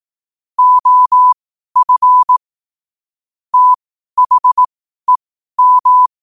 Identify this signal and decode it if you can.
Morse code, heard as a keyed tone, about 18 wpm, spaced as for 12 wpm, 1000 Hz. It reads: OF THEM
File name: qotdfiles / 12wpm